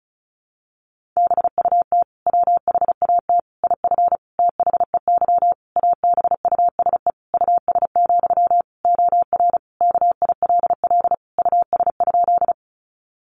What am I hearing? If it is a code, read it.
Morse code, heard as a keyed tone, about 34 wpm, 700 Hz. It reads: BUT WHAT IF THEY ABUSE US, OR KILL US?
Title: Sentences from Top 1400 Words 35wpm